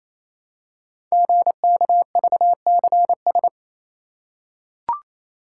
Morse code, heard as a keyed tone, about 27 wpm, 700 Hz. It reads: GKVCS
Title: Code Groups - 5 Letters - 28wpm